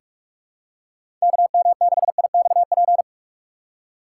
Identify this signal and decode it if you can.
Morse code, heard as a keyed tone, about 42 wpm, 700 Hz. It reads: KM6IXP
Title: USA Call Signs - 2x Character Spacing - 45wpm